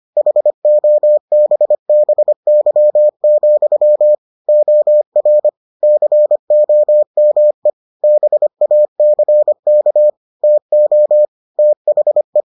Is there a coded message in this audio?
Yes: Morse code from 0.2 to 12.4 s, heard as a keyed tone, about 25 wpm, 600 Hz. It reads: HOBBY, OR COME BACK TO THE